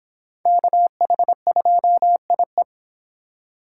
Morse code, heard as a keyed tone, about 25 wpm, 700 Hz. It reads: KH2IE